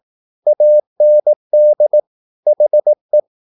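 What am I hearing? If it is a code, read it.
Morse code, heard as a keyed tone, about 18 wpm, 600 Hz. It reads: AND HE
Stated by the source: Morse practice at 18 wpm, English text, from CW med Gnister.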